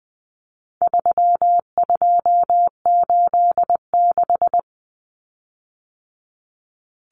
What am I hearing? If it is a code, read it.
Morse code, heard as a keyed tone, about 20 wpm, 700 Hz. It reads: 3286
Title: Common Cut Numbers - 4-Digit 20wpm